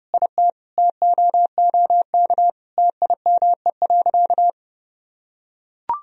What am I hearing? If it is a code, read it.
Morse code, heard as a keyed tone, about 30 wpm, 700 Hz. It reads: IT TOOK TIME.
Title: Sentences from Top 1000 Words - Review Part 4 - 30wpm